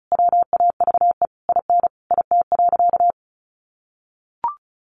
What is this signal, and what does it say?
Morse code, heard as a keyed tone, about 34 wpm, 700 Hz. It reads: WAVE IN IT.
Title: Sentences from Top 1400 Words 35wpm